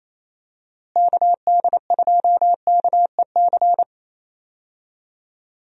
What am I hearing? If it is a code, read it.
Morse code, heard as a keyed tone, about 28 wpm, 700 Hz. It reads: KD2KEC
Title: USA Call Signs - 6 Characters - Repeated 2x - 28wpm